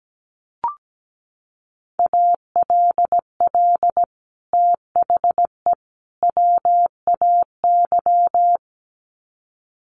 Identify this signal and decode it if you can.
Morse code, heard as a keyed tone, about 17 wpm, 700 Hz. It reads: ALL THE WAY
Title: Sets of 3 Words 17wpm